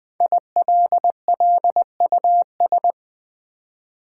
Morse code, heard as a keyed tone, about 20 wpm, 700 Hz. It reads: ILLUS
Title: Build A Word - 5 Letters At A Time - 12-Letter Words - 20wpm